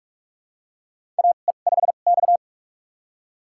Morse code, heard as a keyed tone, about 46 wpm, spaced as for 32 wpm, 700 Hz. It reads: AE5X
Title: USA Call Signs - 4 Characters - 2x Character Spacing - 45wpm